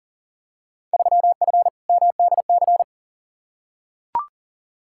Morse code, heard as a keyed tone, about 38 wpm, 700 Hz. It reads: 3F MDC
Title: Field Day Exchanges Encore - 40wpm